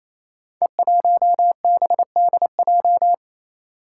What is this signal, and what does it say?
Morse code, heard as a keyed tone, about 28 wpm, 700 Hz. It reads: E1BDJ
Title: Code Groups - 5 Letter:Numbers - 28wpm